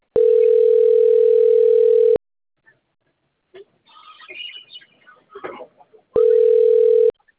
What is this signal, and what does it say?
Ringback tone (North America), 440 + 480 Hz.